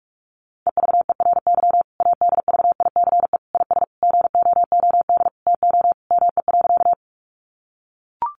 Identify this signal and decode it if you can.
Morse code, heard as a keyed tone, about 44 wpm, 700 Hz. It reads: EVERY ADVICE IS GOOD TO ME.
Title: Sentences from Top 1100 Words 45wpm